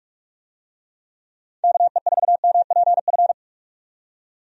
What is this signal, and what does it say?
Morse code, heard as a keyed tone, about 42 wpm, 700 Hz. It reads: KE4MPF
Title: USA Call Signs - 6 Characters - 45wpm